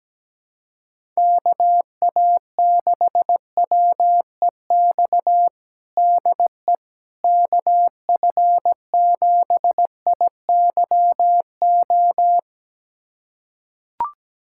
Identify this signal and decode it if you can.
Morse code, heard as a keyed tone, about 17 wpm, 700 Hz. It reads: KA6WEX DE KF7IYO
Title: USA Callsign DE Callsign - 6 Characters - 17wpm